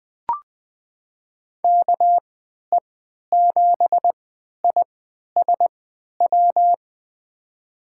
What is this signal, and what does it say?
Morse code, heard as a keyed tone, about 20 wpm, spaced as for 11 wpm, 700 Hz. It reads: KE7ISW
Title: USA Call Signs - 3x Character Spacing - 20wpm